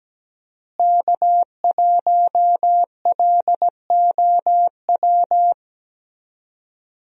Morse code, heard as a keyed tone, about 17 wpm, 700 Hz. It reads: K1LOW
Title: USA Call Signs - 5 Characters - Repeated 3x - 17wpm